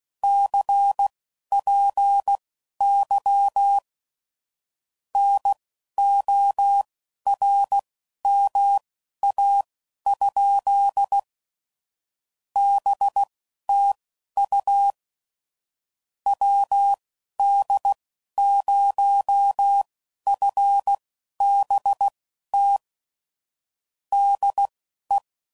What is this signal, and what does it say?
Morse code, heard as a keyed tone, about 16 wpm, spaced as for 11 wpm, 800 Hz. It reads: CPY NORMA? BTU WD0FBT DE
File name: CW Code Groups @15WPM / QSOs / WD0FBT Spaced